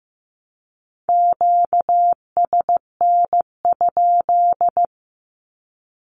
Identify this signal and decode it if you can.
Morse code, heard as a keyed tone, about 15 wpm, 700 Hz. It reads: QSN?
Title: Q-Codes Repeated 3x 15wpm